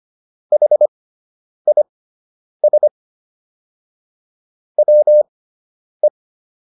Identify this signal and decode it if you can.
Morse code, heard as a keyed tone, about 25 wpm, spaced as for 9 wpm, 600 Hz. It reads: HIS WE